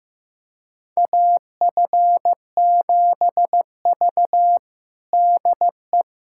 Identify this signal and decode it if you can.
Morse code, heard as a keyed tone, about 16 wpm, 700 Hz. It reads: AF7V DE
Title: USA Callsign DE Callsign - 4 Characters - 15wpm